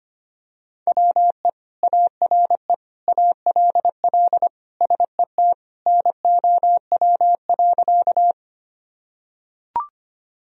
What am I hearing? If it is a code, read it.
Morse code, heard as a keyed tone, about 25 wpm, 700 Hz. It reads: WE ARE ALL SET NOW.
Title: Sentences from Top 1000 Words - Review Part 1 - 25wpm